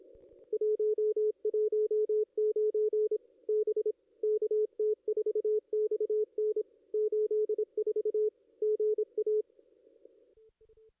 Morse code, heard as a keyed tone, about 27 wpm, 440 Hz. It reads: 119 B KT4XN 84 GA
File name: ss-cw-k5tr-op-n5rz